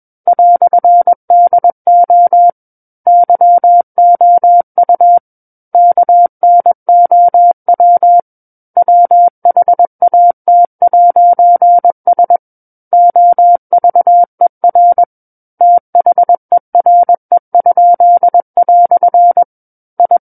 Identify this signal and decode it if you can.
Morse code, heard as a keyed tone, about 21 wpm, 700 Hz. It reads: "DO YOU KNOW WHAT'S OVER THERE?" I
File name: Book One / 12-Chapter 12 - 21 WPM